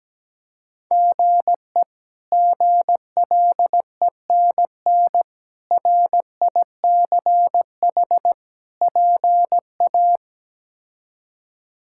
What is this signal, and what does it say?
Morse code, heard as a keyed tone, about 17 wpm, 700 Hz. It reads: GE GLENN RICH PA
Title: K1USN SST QSO Practice - 17wpm